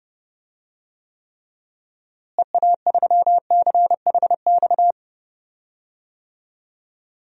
Morse code, heard as a keyed tone, about 30 wpm, 700 Hz. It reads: EA3CHX
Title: DXCC-100 Call Signs - 6 Characters - Repeated 2x - 30wpm